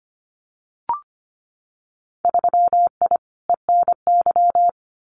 Morse code, heard as a keyed tone, about 25 wpm, 700 Hz. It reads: 3I ENY
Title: Winter Field Day Exchanges Encore 25wpm